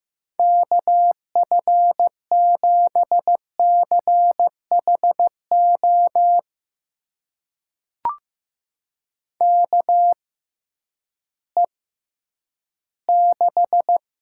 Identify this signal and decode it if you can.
Morse code, heard as a keyed tone, about 15 wpm, spaced as for 10 wpm, 700 Hz. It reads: KF7CHO KE6
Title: USA Call Signs - 6 Characters - 6x Character Spacing - 15wpm